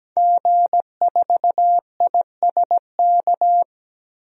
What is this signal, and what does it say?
Morse code, heard as a keyed tone, about 17 wpm, 700 Hz. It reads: G4ISK
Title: DXCC-100 Call Signs - 5 Characters - 17wpm